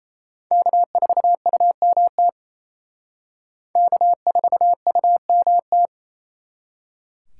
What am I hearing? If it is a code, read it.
Morse code, heard as a keyed tone, about 34 wpm, 700 Hz. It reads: K4UMT K4UMT
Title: Speed Racing - USA 1x3 Call Signs 28wpm